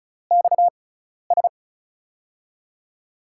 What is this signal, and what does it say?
Morse code, heard as a keyed tone, about 36 wpm, spaced as for 12 wpm, 700 Hz. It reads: XS
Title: USA Call Signs - 4 Characters - 6x Character Spacing - 35wpm